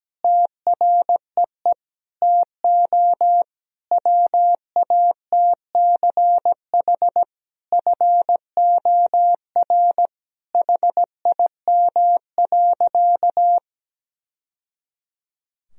Morse code, heard as a keyed tone, about 17 wpm, 700 Hz. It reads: TREE TO WATCH FOR HIM.
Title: Sentences from Top 1000 Words - Review Part 1 - 17wpm